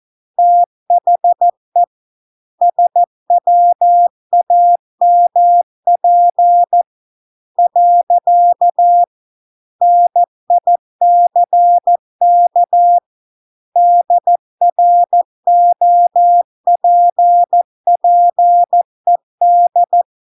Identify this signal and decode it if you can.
Morse code, heard as a keyed tone, about 14 wpm, 700 Hz. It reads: THE SWAMP . NICK DROPPED